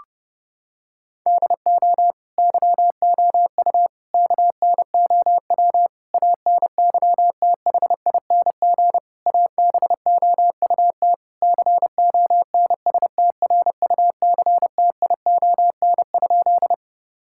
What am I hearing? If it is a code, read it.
Morse code, heard as a keyed tone, about 30 wpm, 700 Hz. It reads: DO YOU KNOW ANYTHING ABOUT CONSTRUCTION?
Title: Sentences from Top 1600 Words 30wpm